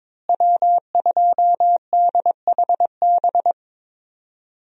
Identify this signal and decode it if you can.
Morse code, heard as a keyed tone, about 21 wpm, 700 Hz. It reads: W2DHB
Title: USA Call Signs - 5 Characters - Repeated 3x - 22wpm